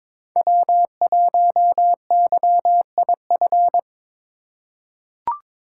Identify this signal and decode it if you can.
Morse code, heard as a keyed tone, about 22 wpm, 700 Hz. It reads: W1YIF
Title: USA Call Signs - 2x Character Spacing - 22wpm